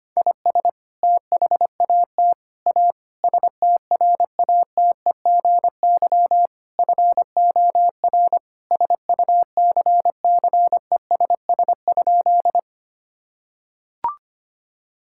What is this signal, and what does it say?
Morse code, heard as a keyed tone, about 25 wpm, 700 Hz. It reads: IS THAT A STRATEGY FOR SUCCESS?
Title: Sentences from Top 1000 Words - Review Part 5 - 25wpm